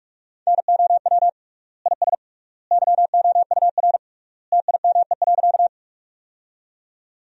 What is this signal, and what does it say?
Morse code, heard as a keyed tone, about 46 wpm, 700 Hz. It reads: NOW IS YOUR TIME.
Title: 3x Word-Spacing - Sentences from Top 100 Words - 45wpm